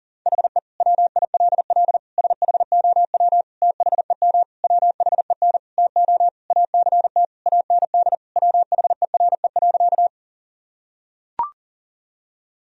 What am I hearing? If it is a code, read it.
Morse code, heard as a keyed tone, about 40 wpm, 700 Hz. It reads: HE WILL SHOW THEM WHEN TO ACT AND WHERE.
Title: Sentences from Top 1000 Words - Review Part 1 - 40wpm